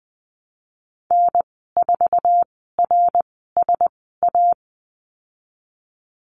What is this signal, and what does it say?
Morse code, heard as a keyed tone, about 19 wpm, spaced as for 14 wpm, 700 Hz. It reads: N4RSA
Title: 076 Lesson 13g - Letter L - Call Signs 20wpm